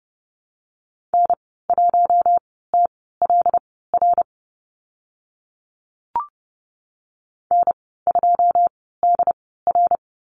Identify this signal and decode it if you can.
Morse code, heard as a keyed tone, about 30 wpm, spaced as for 17 wpm, 690 Hz. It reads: N1TLR N2DR
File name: Lesson 15g - Number 5 - Call Signs 30wpm